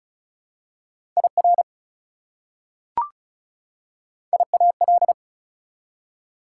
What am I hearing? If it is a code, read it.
Morse code, heard as a keyed tone, about 36 wpm, 700 Hz. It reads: IR IAL
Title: Prefixes and Suffixes 35wpm